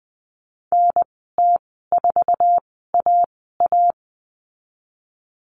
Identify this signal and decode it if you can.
Morse code, heard as a keyed tone, about 20 wpm, spaced as for 15 wpm, 700 Hz. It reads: NT4AA
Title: 048 Lesson 9g - Number 4 - Call Signs 20wpm